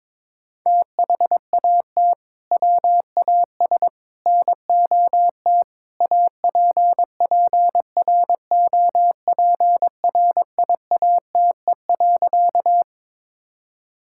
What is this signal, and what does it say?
Morse code, heard as a keyed tone, about 22 wpm, 700 Hz. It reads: THAT WAS NOT APPROPRIATE.
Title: Sentences from Top 1000 Words - Review Part 5 - 22wpm